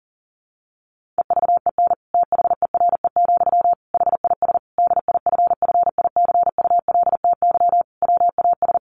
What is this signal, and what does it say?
Morse code, heard as a keyed tone, about 40 wpm, 700 Hz. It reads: EVEN THERE, HIS DIFFICULTY WAS
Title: Sentences from Top 1500 Words 40wpm